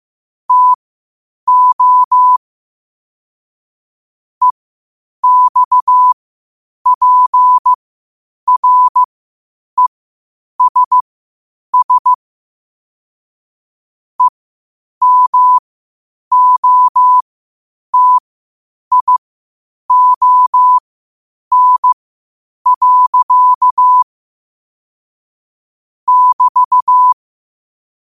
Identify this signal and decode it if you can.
Morse code, heard as a keyed tone, about 15 wpm, spaced as for 8 wpm, 1000 Hz. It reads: TO EXPRESS EMOTION. <BT>
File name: qotdfiles / 8wpm